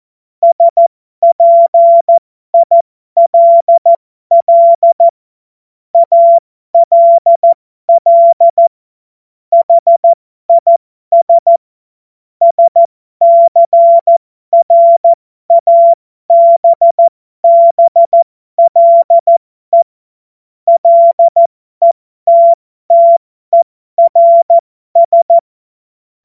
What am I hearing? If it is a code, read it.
Morse code, heard as a keyed tone, about 14 wpm, spaced as for 12 wpm, 680 Hz. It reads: SPILL ALL HIS SCRABBLE LETTERS
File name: Morsekurs / Witze